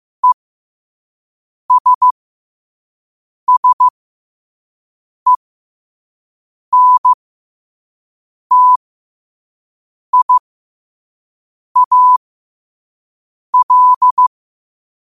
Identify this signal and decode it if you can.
Morse code, heard as a keyed tone, about 15 wpm, spaced as for 5 wpm, 1000 Hz. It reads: ESSENTIAL
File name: qotdfiles / 5wpm